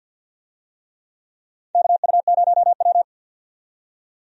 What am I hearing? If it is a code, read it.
Morse code, heard as a keyed tone, about 47 wpm, 700 Hz. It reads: KU0W
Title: USA Call Signs - 4 Characters - 50wpm